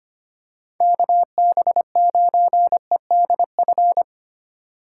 Morse code, heard as a keyed tone, about 25 wpm, 700 Hz. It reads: KB9EDF
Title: USA Call Signs - 6 Characters - 6x Character Spacing - 25wpm